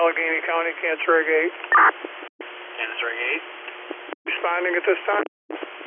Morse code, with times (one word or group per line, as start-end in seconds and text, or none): none